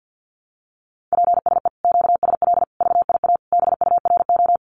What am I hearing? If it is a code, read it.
Morse code, heard as a keyed tone, about 50 wpm, 700 Hz. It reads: PSE QSL VIA BURO
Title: Common QSO Phrases Encore 50wpm